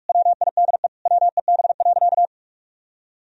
Morse code, heard as a keyed tone, about 46 wpm, 700 Hz. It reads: WIDE WEB.
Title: Sentences from Top 1000 Words - Review Part 4 - 45wpm